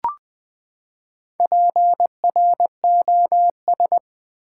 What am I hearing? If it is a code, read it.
Morse code, heard as a keyed tone, about 20 wpm, 700 Hz. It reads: PROS
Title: Build A Word - 4 Letters At A Time - 11-Letter Words - 20wpm